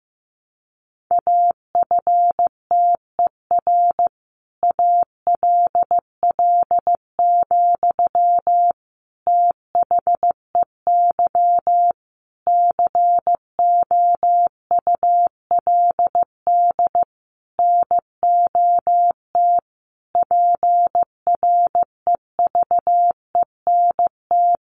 Morse code, heard as a keyed tone, about 15 wpm, 700 Hz. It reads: AFTER ALL, THEY COULD NOT PREVENT
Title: Sentences from Top 1100 Words 15wpm